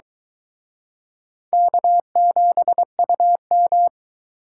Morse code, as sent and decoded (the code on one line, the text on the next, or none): -.- --... ..- --
K7UM